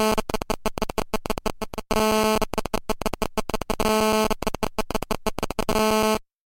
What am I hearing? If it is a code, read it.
There is no code.